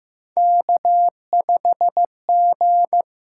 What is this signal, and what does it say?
Morse code, heard as a keyed tone, about 15 wpm, 700 Hz. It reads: K5G
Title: USA Call Signs - 6x Character Spacing - 15wpm